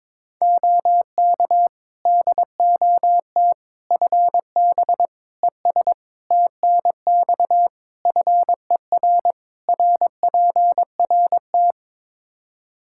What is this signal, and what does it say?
Morse code, heard as a keyed tone, about 21 wpm, 700 Hz. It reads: OK DOT FB ES TNX FER RPRT
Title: Practice QSO - Exchange 2 - Info - 22wpm